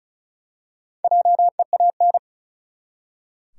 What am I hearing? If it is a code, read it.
Morse code, heard as a keyed tone, about 36 wpm, 700 Hz. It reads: JEAN
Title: USA Common Names - Max 5 Letters - 35wpm